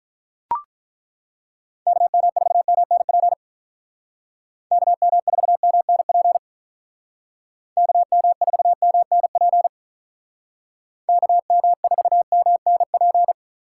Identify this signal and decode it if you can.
Morse code, heard as a keyed tone, about 42 wpm, 700 Hz. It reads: KM4MNP KM4MNP KM4MNP KM4MNP
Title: Speed Racing - USA Call Signs 35wpm